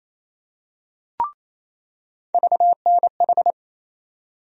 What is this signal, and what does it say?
Morse code, heard as a keyed tone, about 27 wpm, 700 Hz. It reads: VNH
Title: USA Call Sign Postfixes - 3 Letters 28wpm